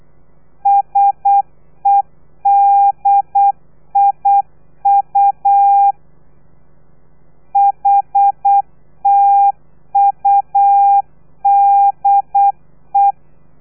Morse code, heard as a keyed tone, about 8 wpm, 800 Hz. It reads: SEDIU HTUDE
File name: telegrafia / i2tfj / Lez3 35